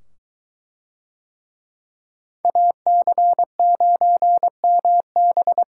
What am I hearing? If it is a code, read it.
Morse code, heard as a keyed tone, about 21 wpm, 700 Hz. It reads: AC9MB